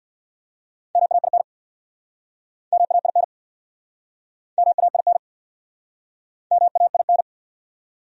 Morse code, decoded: MAIN MAIN MAIN MAIN

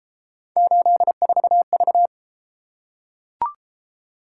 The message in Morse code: 84V

